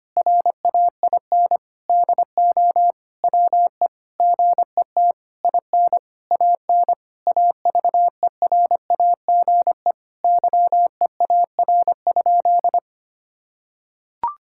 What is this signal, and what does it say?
Morse code, heard as a keyed tone, about 25 wpm, 700 Hz. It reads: RAIN DO WE GET IN AN AVERAGE YEAR?
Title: Sentences from Top 1000 Words - Review Part 6 - 25wpm